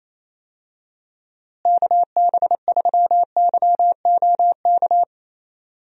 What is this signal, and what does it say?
Morse code, heard as a keyed tone, about 28 wpm, 700 Hz. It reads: KB3YOK
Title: USA Call Signs - 6 Characters - Repeated 3x - 28wpm